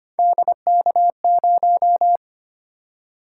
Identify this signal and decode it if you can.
Morse code, heard as a keyed tone, about 25 wpm, 700 Hz. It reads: DK0
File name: Three Letter-Number Combinations - 25wpm